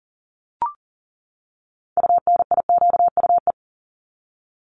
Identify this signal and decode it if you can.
Morse code, heard as a keyed tone, about 40 wpm, 700 Hz. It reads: UNIQUE
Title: Top 751-1000 Words 40wpm